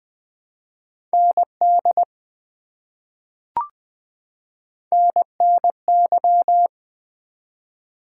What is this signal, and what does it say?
Morse code, heard as a keyed tone, about 20 wpm, 700 Hz. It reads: ND NNY
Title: ARRL Sections v2 - 20wpm